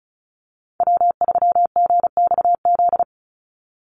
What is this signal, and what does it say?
Morse code, heard as a keyed tone, about 34 wpm, 700 Hz. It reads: W3GXZ
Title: USA Call Signs 35wpm